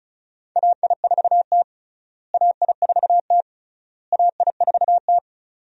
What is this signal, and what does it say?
Morse code, heard as a keyed tone, about 36 wpm, 700 Hz. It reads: AI4T AI4T AI4T
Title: USA Call Signs - 4 Characters - Repeated 3x - 35wpm